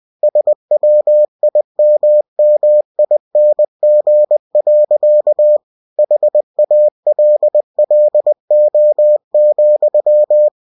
Morse code, heard as a keyed tone, about 20 wpm, 600 Hz. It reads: SWIMMING. HALLO,
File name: cw-books / winnie-the-pooh / winnie-20wpm-page-0087